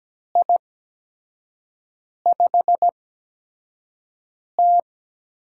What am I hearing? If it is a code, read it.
Morse code, heard as a keyed tone, about 17 wpm, spaced as for 5 wpm, 700 Hz. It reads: I5T
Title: USA Call Signs - 8x Character Spacing - 17wpm